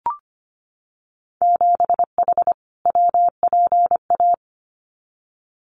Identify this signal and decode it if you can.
Morse code, heard as a keyed tone, about 25 wpm, 700 Hz. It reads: 7H WPA
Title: Winter Field Day Exchanges Encore 25wpm